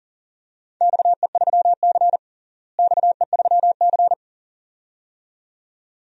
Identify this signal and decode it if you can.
Morse code, heard as a keyed tone, about 40 wpm, 700 Hz. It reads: XE3C XE3C
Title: DXCC-100 Call Signs - 4 Characters - Repeated 2x - 40wpm